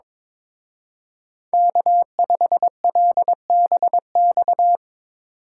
Morse code, heard as a keyed tone, about 24 wpm, 700 Hz. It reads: K5LBX
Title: Speed Racing - USA 1x3 Call Signs 22wpm